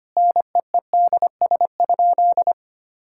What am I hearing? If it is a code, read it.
Morse code, heard as a keyed tone, about 25 wpm, 700 Hz. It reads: NEEDS?
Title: Sentences from Top 1000 Words - Review Part 5 - 25wpm